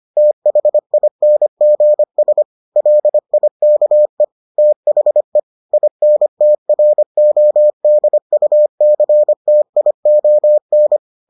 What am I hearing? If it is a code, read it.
Morse code, heard as a keyed tone, about 25 wpm, 600 Hz. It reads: THINGS LIKE THE INTRODUCTION